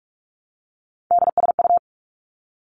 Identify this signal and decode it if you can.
Morse code, heard as a keyed tone, about 46 wpm, 700 Hz. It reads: DSU
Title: Three Challenging Letter-Number Combinations 45wpm